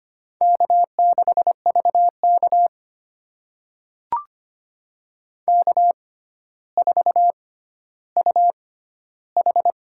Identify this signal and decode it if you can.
Morse code, heard as a keyed tone, about 25 wpm, spaced as for 13 wpm, 700 Hz. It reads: K6VK K4UH
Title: USA Call Signs - 4 Characters - 6x Character Spacing - 25wpm